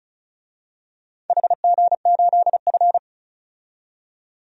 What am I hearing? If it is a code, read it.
Morse code, heard as a keyed tone, about 36 wpm, 700 Hz. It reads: HG8F
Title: DXCC-100 Call Signs - 4 Characters - 35wpm